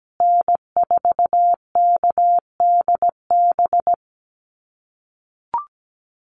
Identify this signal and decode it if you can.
Morse code, heard as a keyed tone, about 17 wpm, 700 Hz. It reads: N4KDB
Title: USA Call Signs Repeated 2x 17wpm